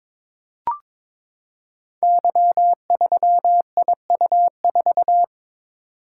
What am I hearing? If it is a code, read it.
Morse code, heard as a keyed tone, about 23 wpm, 700 Hz. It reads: Y3IU4
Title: Code Groups - 5 Letter:Numbers - 22wpm